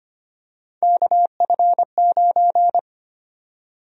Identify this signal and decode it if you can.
Morse code, heard as a keyed tone, about 25 wpm, 700 Hz. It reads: KF9
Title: USA Call Sign Prefixes - 25wpm